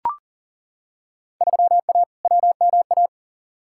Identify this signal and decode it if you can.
Morse code, heard as a keyed tone, about 40 wpm, 700 Hz. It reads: 3A WMA
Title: Field Day Exchanges - 40wpm